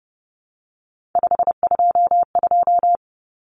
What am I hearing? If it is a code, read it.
Morse code, heard as a keyed tone, about 30 wpm, 700 Hz. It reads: 522
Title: Common Cut Numbers - 3-Digit 30wpm